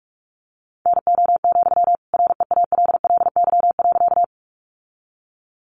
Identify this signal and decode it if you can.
Morse code, heard as a keyed tone, about 44 wpm, 700 Hz. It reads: NO, REALLY.
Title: Sentences from Top 600 Words 45wpm